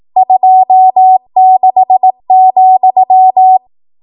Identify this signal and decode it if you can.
Morse code, heard as a keyed tone, about 18 wpm, 750 Hz. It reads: 26,
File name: Archive / 18 WPM / 18WPM